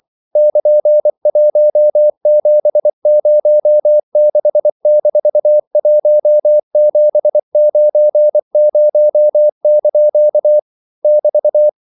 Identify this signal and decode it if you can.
Morse code, heard as a keyed tone, about 24 wpm, 600 Hz. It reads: <KN>1706-1790) <BT>
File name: CW / Practice / EnglishQuotations0424